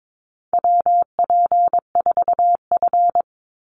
Morse code, heard as a keyed tone, about 22 wpm, 700 Hz. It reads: WP4F
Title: Call Signs - 22wpm